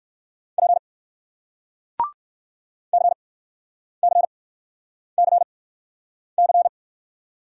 Morse code, heard as a keyed tone, about 54 wpm, spaced as for 9 wpm, 700 Hz. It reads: P CCCC
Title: Speed-Racing Single Letter-Number 45wpm